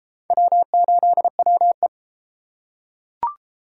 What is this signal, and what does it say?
Morse code, heard as a keyed tone, about 31 wpm, 700 Hz. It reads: W8WE